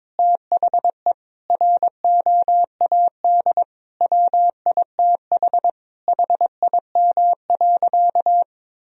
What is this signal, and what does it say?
Morse code, heard as a keyed tone, about 22 wpm, 700 Hz. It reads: THE ROAD WITH HIM.
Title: Sentences from Top 1000 Words - Review Part 2 - 22wpm